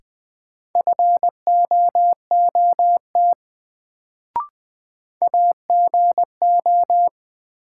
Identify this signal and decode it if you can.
Morse code, heard as a keyed tone, about 21 wpm, 700 Hz. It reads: FOOT AGO
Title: Sentences from Top 500 Words - New Word Drill - 20wpm